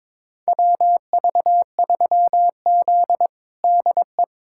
Morse code, heard as a keyed tone, about 22 wpm, 700 Hz. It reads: WV3Z DE
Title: USA Callsign DE Callsign - 4 Characters - 22wpm